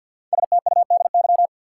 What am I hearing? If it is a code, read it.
Morse code, heard as a keyed tone, about 53 wpm, 700 Hz. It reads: STUDY
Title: Sentences from Top 300 Words - New Word Drill - 50wpm